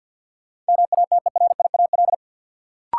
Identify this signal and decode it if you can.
Morse code, heard as a keyed tone, about 53 wpm, 700 Hz. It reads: MATERIAL
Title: Sentences from Top 1100 Words - New Word Drill Alt - 50wpm